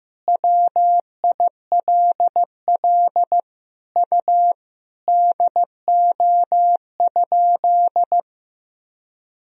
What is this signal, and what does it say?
Morse code, heard as a keyed tone, about 16 wpm, 700 Hz. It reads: WILL U DO?
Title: Sentences from Top 100 Words and Abbreviations 15wpm